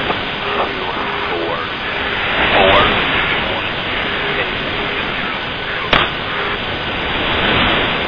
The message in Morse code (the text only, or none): none